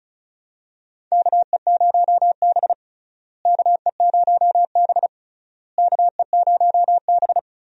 Morse code, heard as a keyed tone, about 36 wpm, 700 Hz. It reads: KE0B KE0B KE0B
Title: USA Call Signs - 4 Characters - Repeated 3x - 35wpm